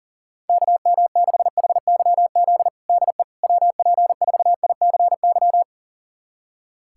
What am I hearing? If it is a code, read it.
Morse code, heard as a keyed tone, about 40 wpm, 700 Hz. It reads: KM6HYZ DE WP4ICY
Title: USA Callsign DE Callsign - 6 Characters - 40wpm